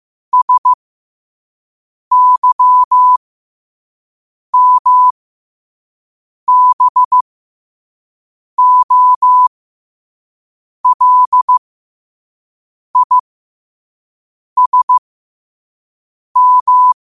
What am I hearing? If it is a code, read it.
Morse code, heard as a keyed tone, about 15 wpm, spaced as for 5 wpm, 1000 Hz. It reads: SYMBOLISM
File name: qotdfiles / 5wpm